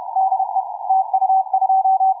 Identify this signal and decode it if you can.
Morse code, heard as a keyed tone, about 31 wpm, 750 Hz. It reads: DE TU2
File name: cw forum / tu2 f5ldy 18mhz 16h46